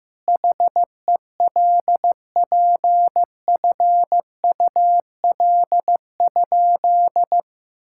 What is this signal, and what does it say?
Morse code, heard as a keyed tone, about 15 wpm, 700 Hz. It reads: HELPFUL?
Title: Sentences from Top 1000 Words - Review Part 6 - 15wpm